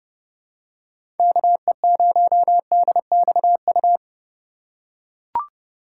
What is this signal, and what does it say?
Morse code, heard as a keyed tone, about 30 wpm, 700 Hz. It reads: KE0DXU
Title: USA Call Signs - 6 Characters - 3x Character Spacing - 30wpm